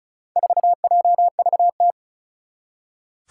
Morse code, heard as a keyed tone, about 36 wpm, 700 Hz. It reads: 4JVT
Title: Four Challenging Letter-Number Combinations 35wpm